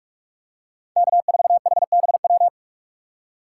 Morse code, heard as a keyed tone, about 48 wpm, 700 Hz. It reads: K4HBW
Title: USA Call Signs - 5 Characters - 4x Character Spacing - 45wpm